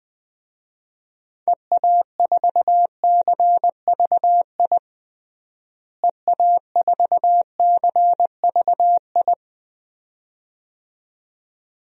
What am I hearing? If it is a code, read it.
Morse code, heard as a keyed tone, about 21 wpm, 700 Hz. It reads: EA4CVI EA4CVI
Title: DXCC-100 Call Signs - 6 Characters - Repeated 2x - 20wpm